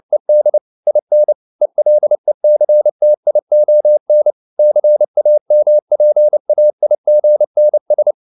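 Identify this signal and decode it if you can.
Morse code, heard as a keyed tone, about 29 wpm, 600 Hz. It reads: ED IN ELECTION CAMPAIGNS
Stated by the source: Morse code at 29 wpm, sending Ken Follett's novel Never in English, from CW med Gnister.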